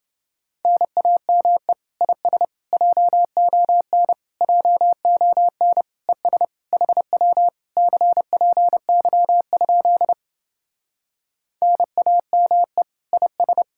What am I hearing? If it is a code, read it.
Morse code, heard as a keyed tone, about 30 wpm, 700 Hz. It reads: NAME IS JON JON ES HW CPY? NAME IS